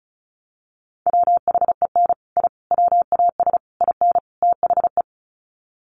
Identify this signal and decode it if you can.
Morse code, heard as a keyed tone, about 34 wpm, 700 Hz. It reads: WHEN I WAS IN THE
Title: Sets of 5 Words Encore2 35wpm